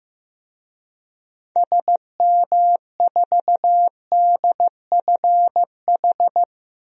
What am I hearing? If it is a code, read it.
Morse code, heard as a keyed tone, about 15 wpm, 700 Hz. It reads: SM4DFH
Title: DXCC-100 Call Signs - 6 Characters - 15wpm